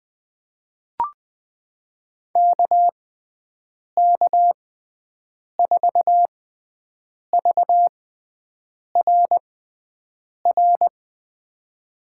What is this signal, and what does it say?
Morse code, heard as a keyed tone, about 21 wpm, spaced as for 7 wpm, 700 Hz. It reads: KK4VRR
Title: USA Call Signs - 6 Characters - 6x Character Spacing - 20wpm